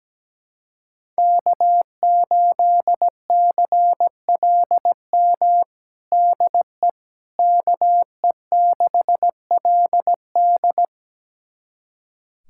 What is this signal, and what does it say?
Morse code, heard as a keyed tone, about 17 wpm, 700 Hz. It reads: K8CLM DE KE6LD
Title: USA Callsign DE Callsign - 5 Characters - 17wpm